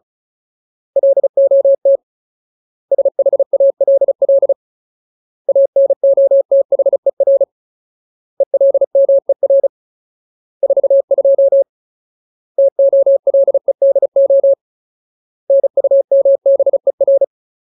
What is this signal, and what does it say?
Morse code, heard as a keyed tone, about 34 wpm, 550 Hz. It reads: LOT SHALL ANOTHER ELMER 42 TOLEDO NUMBER
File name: Ultimate Word List / 35WPM Ultimate